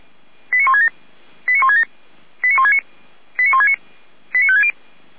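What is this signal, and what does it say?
Five-tone selective call (ZVEI), heard as quick runs of five tones, tones of about 70 ms. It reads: 89167 89167 89169 89169 89560
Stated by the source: ZVEI five-tone call received by SDR, from audio.f85360.feuerwehr-orlamuende.org (Feuerwehr Orlamünde).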